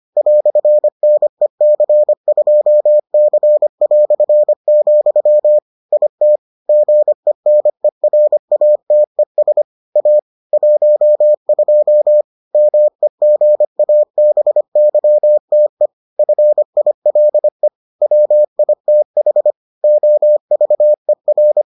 Morse code, heard as a keyed tone, about 25 wpm, 600 Hz. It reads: "NEC2C", IT GENERATES A 12 MEGABYTE FILE WITH OVER